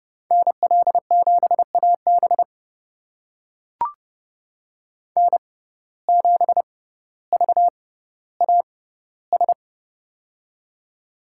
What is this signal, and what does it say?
Morse code, heard as a keyed tone, about 30 wpm, spaced as for 15 wpm, 700 Hz. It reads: NL7AB N7VAS
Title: USA Call Signs - 5 Characters - 6x Character Spacing - 30wpm